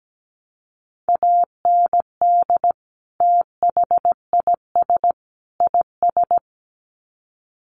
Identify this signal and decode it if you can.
Morse code, heard as a keyed tone, about 17 wpm, 700 Hz. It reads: AND THIS IS
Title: Sets of 3 Words 17wpm